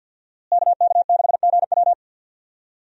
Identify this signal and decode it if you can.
Morse code, heard as a keyed tone, about 50 wpm, 700 Hz. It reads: KK6GW
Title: USA Call Signs - 5 Characters - Repeated 3x - 50wpm